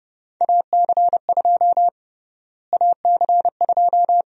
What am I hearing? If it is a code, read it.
Morse code, heard as a keyed tone, about 30 wpm, 700 Hz. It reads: AC2 AC2
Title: USA Call Sign Prefixes - Repeated 2x - 30wpm